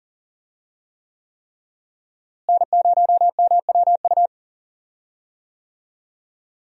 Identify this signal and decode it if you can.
Morse code, heard as a keyed tone, about 40 wpm, 700 Hz. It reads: N0MWU
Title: USA Call Signs - 5 Characters - Repeated 3x - 40wpm